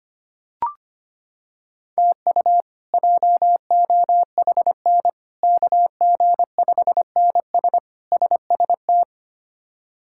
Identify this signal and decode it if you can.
Morse code, heard as a keyed tone, about 25 wpm, 700 Hz. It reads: TU JOHN KG5NS SST
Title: K1USN SST QSO Practice - 25wpm